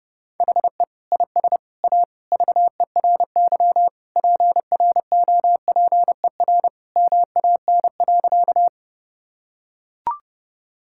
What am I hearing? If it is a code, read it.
Morse code, heard as a keyed tone, about 30 wpm, 700 Hz. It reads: HE IS A VERY PROPER MAN.
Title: Sentences from Top 1000 Words - Review Part 5 - 30wpm